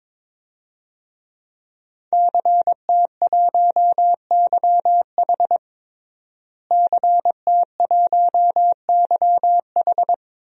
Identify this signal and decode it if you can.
Morse code, heard as a keyed tone, about 22 wpm, 700 Hz. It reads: CT1YH CT1YH
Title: DXCC-100 Call Signs - 5 Characters - Repeated 2x - 22wpm